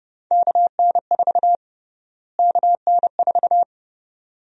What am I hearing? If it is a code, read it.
Morse code, heard as a keyed tone, about 30 wpm, 700 Hz. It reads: KN4 KN4
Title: USA Call Sign Prefixes - Repeated 2x - 30wpm